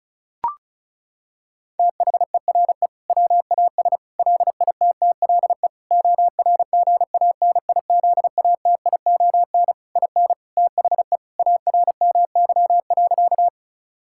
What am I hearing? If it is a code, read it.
Morse code, heard as a keyed tone, about 36 wpm, 700 Hz. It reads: THERE WAS LITTLE ORGANIZATION IN THE ARMY.
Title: Sentences from Top 1000 Words - Review Part 5 - 35wpm